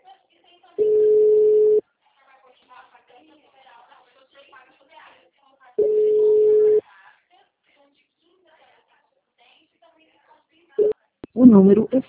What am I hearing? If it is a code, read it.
Ringback tone (Europe, 425 Hz), with speech.